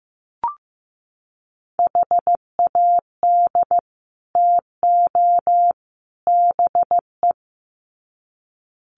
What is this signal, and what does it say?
Morse code, heard as a keyed tone, about 15 wpm, 700 Hz. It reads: HAD TO BE
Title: Sets of 3 Words 15wpm